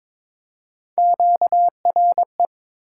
Morse code, heard as a keyed tone, about 22 wpm, 700 Hz. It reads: QRE
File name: practice / 22wpm / Three Letter-Number Combinations - 22wpm